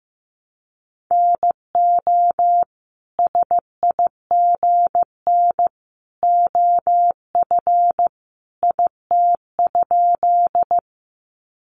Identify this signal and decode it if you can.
Morse code, heard as a keyed tone, about 15 wpm, 700 Hz. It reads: NO SIGN OF IT?
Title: Sentences from Top 1000 Words 15wpm